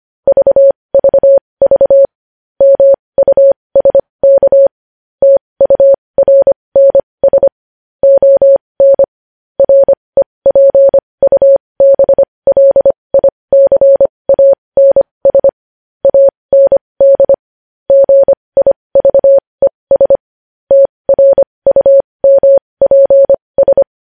Morse code, heard as a keyed tone, about 25 wpm, spaced as for 20 wpm, 550 Hz. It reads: VVV MUSK TURNS ON REPUBLICANS AND GIVES TRUMPS
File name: podcast / News headlines-1x-25x20